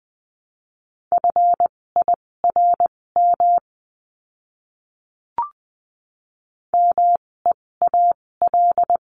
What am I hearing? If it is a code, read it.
Morse code, heard as a keyed tone, about 20 wpm, spaced as for 16 wpm, 700 Hz. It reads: FIRM MEAL
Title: 256 Lesson 42a - Words 12-20wpm